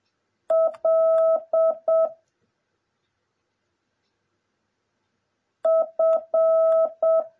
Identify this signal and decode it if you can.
Morse code, heard as a keyed tone, about 7 wpm, 650 Hz. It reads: L F
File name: jh8cbh / 2021cw / cw09a